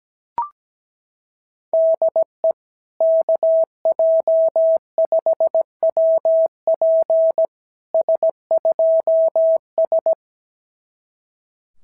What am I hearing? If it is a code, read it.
Morse code, heard as a keyed tone, about 17 wpm, 650 Hz. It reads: DE KJ5WP S2S
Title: SOTA QSO Practice 17wpm